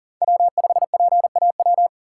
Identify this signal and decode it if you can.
Morse code, heard as a keyed tone, about 40 wpm, 700 Hz. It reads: W5PAW
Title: USA Call Signs - 5 Characters - 3x Character Spacing - 40wpm